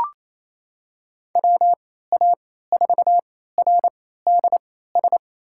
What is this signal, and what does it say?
Morse code, heard as a keyed tone, about 27 wpm, spaced as for 16 wpm, 700 Hz. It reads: WA4RDS
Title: USA Call Signs - 6 Characters - 3x Character Spacing - 28wpm